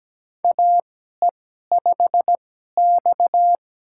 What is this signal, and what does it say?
Morse code, heard as a keyed tone, about 17 wpm, spaced as for 12 wpm, 700 Hz. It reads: AE5X
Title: USA Call Signs - 4 Characters - 2x Character Spacing - 17wpm